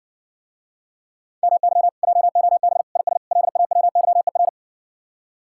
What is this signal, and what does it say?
Morse code, heard as a keyed tone, about 60 wpm, 700 Hz. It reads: MY JOB IS LAWYER